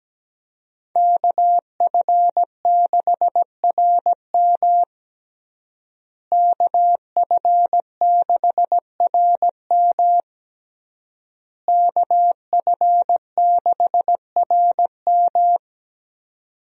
Morse code, heard as a keyed tone, about 17 wpm, 700 Hz. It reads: KF6RM KF6RM KF6RM
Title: USA Call Signs - 5 Characters - Repeated 3x - 17wpm